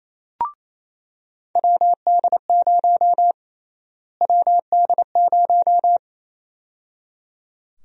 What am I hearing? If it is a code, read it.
Morse code, heard as a keyed tone, about 28 wpm, 700 Hz. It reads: WD0 WD0
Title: USA Call Sign Prefixes - Repeated 2x - 28wpm